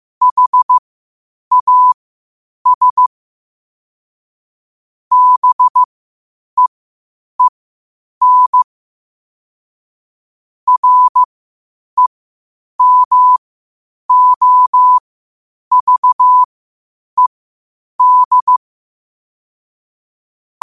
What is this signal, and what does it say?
Morse code, heard as a keyed tone, about 15 wpm, spaced as for 8 wpm, 1000 Hz. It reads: HAS BEEN REMOVED
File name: qotdfiles / 8wpm